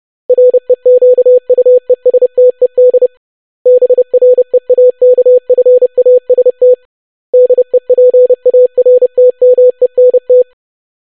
Morse code, heard as a keyed tone, about 30 wpm, 500 Hz. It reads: REQUESTED BREAKFAST DEPARTMENT
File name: longwords-25wpm